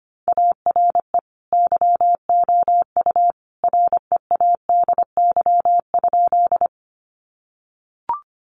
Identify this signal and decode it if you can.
Morse code, heard as a keyed tone, about 25 wpm, 700 Hz. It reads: ARE YOU READY?
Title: Sentences from Top 400 Words 25wpm